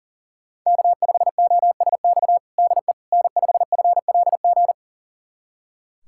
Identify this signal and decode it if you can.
Morse code, heard as a keyed tone, about 40 wpm, 690 Hz. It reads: K5OSX DE N5FLG
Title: USA Callsign DE Callsign - 5 Characters - 40wpm